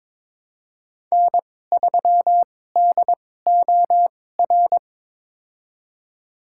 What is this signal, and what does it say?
Morse code, heard as a keyed tone, about 22 wpm, spaced as for 16 wpm, 700 Hz. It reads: N3DOR
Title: USA Call Signs - 5 Characters - 2x Character Spacing - 22wpm